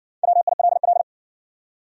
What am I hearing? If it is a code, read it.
Morse code, heard as a keyed tone, about 57 wpm, 700 Hz. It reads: WILL